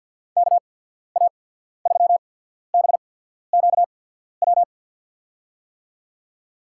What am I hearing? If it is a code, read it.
Morse code, heard as a keyed tone, about 53 wpm, spaced as for 14 wpm, 700 Hz. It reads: KA3BQW
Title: USA Call Signs - 6 Characters - 8x Character Spacing - 50wpm